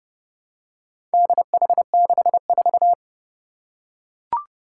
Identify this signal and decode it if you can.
Morse code, heard as a keyed tone, about 30 wpm, 700 Hz. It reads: DH64
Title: Four Challenging Letter-Number Combinations 30wpm